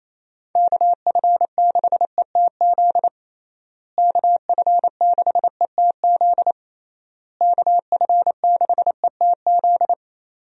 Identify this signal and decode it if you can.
Morse code, heard as a keyed tone, about 28 wpm, 700 Hz. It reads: KF6ETZ KF6ETZ KF6ETZ
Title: USA Call Signs - 6 Characters - Repeated 3x - 28wpm